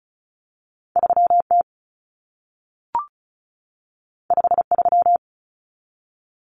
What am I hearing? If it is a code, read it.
Morse code, heard as a keyed tone, about 36 wpm, 700 Hz. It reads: 3T 53
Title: Common Cut Numbers - 2-Digit 35wpm